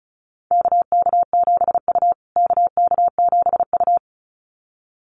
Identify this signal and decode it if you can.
Morse code, heard as a keyed tone, about 34 wpm, 700 Hz. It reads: KK7U KK7U
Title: USA Call Signs Repeated 2x 35wpm